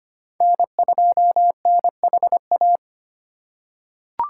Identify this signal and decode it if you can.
Morse code, heard as a keyed tone, about 25 wpm, 700 Hz. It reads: N2NHA
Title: USA Call Signs - 2x Character Spacing - 25wpm